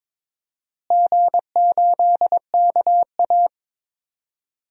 Morse code, heard as a keyed tone, about 22 wpm, 700 Hz. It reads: G8KA
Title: DXCC-100 Call Signs - 4 Characters - Repeated 3x - 22wpm